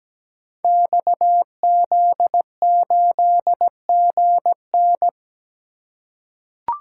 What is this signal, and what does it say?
Morse code, heard as a keyed tone, about 17 wpm, 700 Hz. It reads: XZ8GN
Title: Code Groups - 5 Letter:Numbers - 17wpm